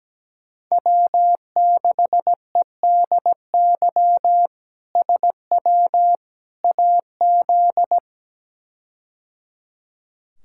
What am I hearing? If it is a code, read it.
Morse code, heard as a keyed tone, about 17 wpm, 700 Hz. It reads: W6EDY SW AZ
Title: K1USN SST QSO Practice - 17wpm